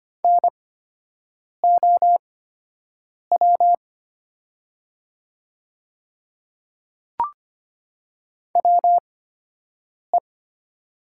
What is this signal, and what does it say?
Morse code, heard as a keyed tone, about 25 wpm, spaced as for 7 wpm, 700 Hz. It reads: NOW WE
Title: 8x Character Spacing - Top 100 Words - 25wpm